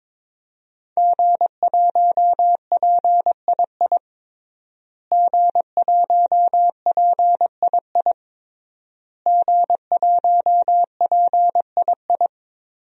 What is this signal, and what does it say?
Morse code, heard as a keyed tone, about 22 wpm, 700 Hz. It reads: G1PII G1PII G1PII
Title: DXCC-100 Call Signs - 5 Characters - Repeated 3x - 22wpm